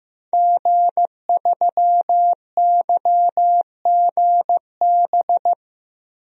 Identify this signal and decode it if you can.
Morse code, heard as a keyed tone, about 15 wpm, 700 Hz. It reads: G3YGB
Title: DXCC-100 Call Signs - 5 Characters - 15wpm